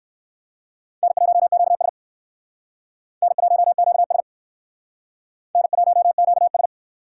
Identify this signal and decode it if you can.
Morse code, heard as a keyed tone, about 58 wpm, 700 Hz. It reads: N1QS N1QS N1QS